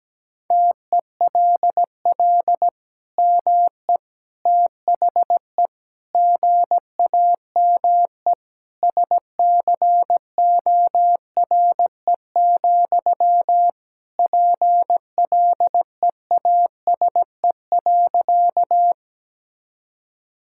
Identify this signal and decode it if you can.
Morse code, heard as a keyed tone, about 17 wpm, 700 Hz. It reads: TELL ME THE GAME SCORE, PLEASE.
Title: Sentences from Top 1600 Words 17wpm